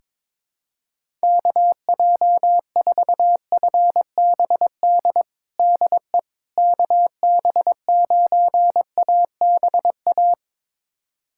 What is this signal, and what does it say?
Morse code, heard as a keyed tone, about 22 wpm, 700 Hz. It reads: KJ4FBD DE KB9ABA
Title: USA Callsign DE Callsign - 6 Characters - 22wpm